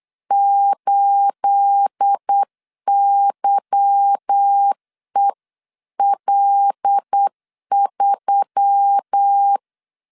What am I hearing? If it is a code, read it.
Morse code, heard as a keyed tone, about 8 wpm, 800 Hz. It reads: 8YEL3